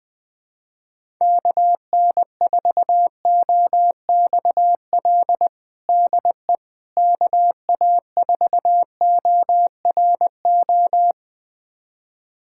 Morse code, heard as a keyed tone, about 21 wpm, 700 Hz. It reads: KN4OXL DE KA4ORO
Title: USA Callsign DE Callsign 20wpm